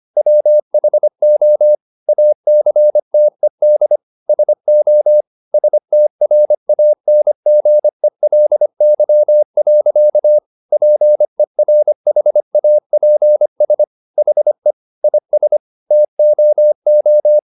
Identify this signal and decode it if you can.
Morse code, heard as a keyed tone, about 25 wpm, 600 Hz. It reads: WHO ACTED SO STRANGELY. PERHAPS HE IS TOO
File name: the old man and the sea part 1 25wpm